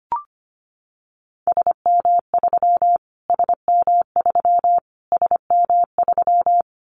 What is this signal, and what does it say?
Morse code, heard as a keyed tone, about 25 wpm, 700 Hz. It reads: SM3 SM3 SM3
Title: Callsign Prefixes for Top 25 DXCC - Repeated 3x - 25wpm